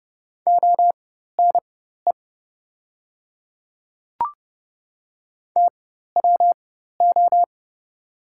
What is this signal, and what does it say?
Morse code, heard as a keyed tone, about 30 wpm, spaced as for 14 wpm, 700 Hz. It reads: ONE TWO
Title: 4x Character Spacing - Top 100 Words - 30wpm